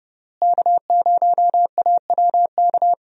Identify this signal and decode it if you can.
Morse code, heard as a keyed tone, about 30 wpm, 700 Hz. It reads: K0AWK
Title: USA Call Signs - 5 Characters - Repeated 3x - 30wpm